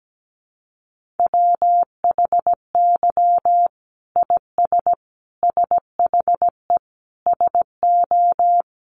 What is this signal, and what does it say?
Morse code, heard as a keyed tone, about 17 wpm, 700 Hz. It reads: WHY IS SHE SO
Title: Sentences from Top 1200 Words 17wpm